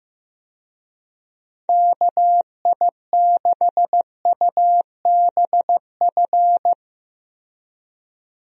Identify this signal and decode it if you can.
Morse code, heard as a keyed tone, about 15 wpm, 700 Hz. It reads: KI6UBF
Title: USA Call Signs - 6 Characters - Repeated 2x - 15wpm